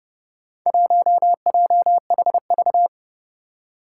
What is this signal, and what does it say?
Morse code, heard as a keyed tone, about 29 wpm, 700 Hz. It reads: 1JHV
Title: Four Challenging Letter-Number Combinations 30wpm